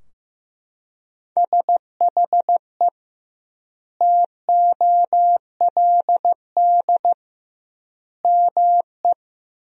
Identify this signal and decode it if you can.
Morse code, heard as a keyed tone, about 15 wpm, 700 Hz. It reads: SHE TOLD ME
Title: 2x Word-Spacing - Sentences from Top 400 Words - 15wpm